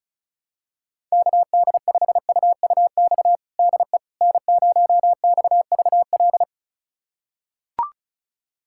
Morse code, heard as a keyed tone, about 34 wpm, 700 Hz. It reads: KD5UUX DE N0XVL
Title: USA Callsign DE Callsign 35wpm